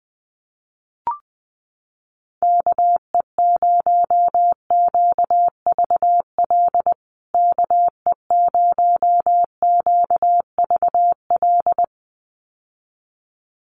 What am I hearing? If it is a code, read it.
Morse code, heard as a keyed tone, about 20 wpm, 700 Hz. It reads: KE0QVL KE0QVL
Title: USA Call Signs Repeated 2x 20wpm